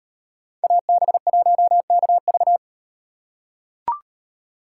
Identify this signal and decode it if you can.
Morse code, heard as a keyed tone, about 38 wpm, 700 Hz. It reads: AB1KV